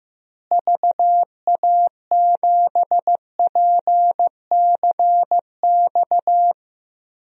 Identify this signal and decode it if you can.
Morse code, heard as a keyed tone, about 15 wpm, 700 Hz. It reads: VA7PCX
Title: DXCC-100 Call Signs - 6 Characters - Repeated 3x - 15wpm